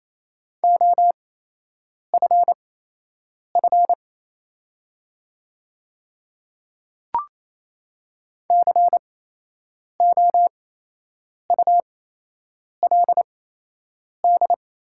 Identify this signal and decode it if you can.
Morse code, heard as a keyed tone, about 27 wpm, spaced as for 8 wpm, 690 Hz. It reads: OFF COULD
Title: 8x Character Spacing - Top 200 Words - 28wpm